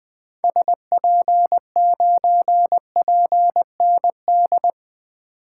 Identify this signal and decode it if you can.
Morse code, heard as a keyed tone, about 20 wpm, 700 Hz. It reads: SP9PND
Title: 2023 SOTA Activator Call Signs - 20wpm